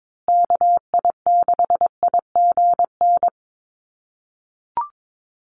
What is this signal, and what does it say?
Morse code, heard as a keyed tone, about 22 wpm, 700 Hz. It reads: KI6IGN
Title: USA Call Signs Repeated 2x 22wpm